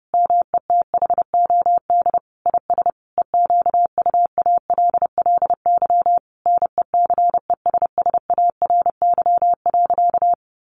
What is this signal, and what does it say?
Morse code, heard as a keyed tone, about 30 wpm, 700 Hz. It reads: METHOD IS EQUALLY NECESSARY.
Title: Sentences from Top 1400 Words 30wpm